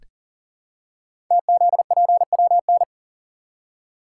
Morse code, heard as a keyed tone, about 39 wpm, 700 Hz. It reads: TZPWN
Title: Code Groups - 5 Letter:Numbers - 40wpm